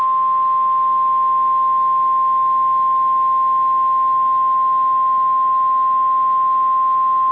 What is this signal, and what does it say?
Emergency Alert System (SAME) tones (the 1050 Hz attention tone). It is heard from 0.8 to 6.7 s.